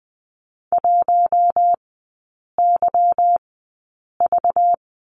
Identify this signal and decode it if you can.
Morse code, heard as a keyed tone, about 20 wpm, spaced as for 8 wpm, 690 Hz. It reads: 1YV
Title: 191 Lesson 31e - Letter v - Three Characters 20wpm